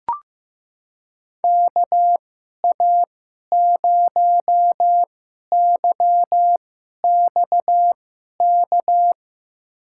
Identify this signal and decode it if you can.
Morse code, heard as a keyed tone, about 15 wpm, spaced as for 11 wpm, 700 Hz. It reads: KA0YXK
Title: USA Call Signs - 6 Characters - 2x Character Spacing - 15wpm